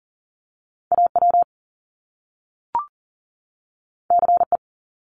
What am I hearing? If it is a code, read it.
Morse code, heard as a keyed tone, about 41 wpm, 700 Hz. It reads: AW CE
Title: Top Two Letter Combinations in Words 40wpm